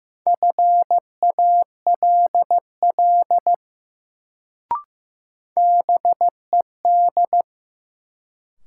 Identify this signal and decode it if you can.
Morse code, heard as a keyed tone, about 16 wpm, 700 Hz. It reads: FALL BED
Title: Sentences from Top 500 Words - New Word Drill - 15wpm